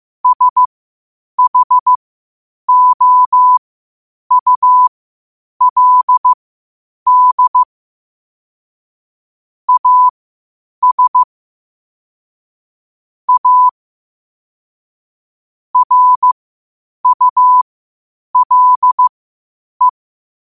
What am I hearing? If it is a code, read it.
Morse code, heard as a keyed tone, about 15 wpm, spaced as for 8 wpm, 1000 Hz. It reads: SHOULD AS A RULE
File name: qotdfiles / 8wpm